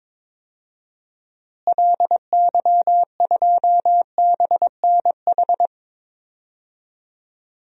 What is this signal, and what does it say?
Morse code, heard as a keyed tone, about 22 wpm, 700 Hz. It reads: LY2BNH
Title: DXCC-100 Call Signs - 6 Characters - Repeated 3x - 22wpm